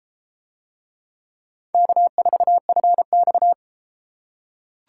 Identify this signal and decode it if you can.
Morse code, heard as a keyed tone, about 30 wpm, 700 Hz. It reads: K4FX